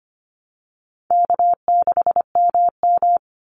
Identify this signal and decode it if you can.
Morse code, heard as a keyed tone, about 25 wpm, 700 Hz. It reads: K6MM
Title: Call Signs - 25wpm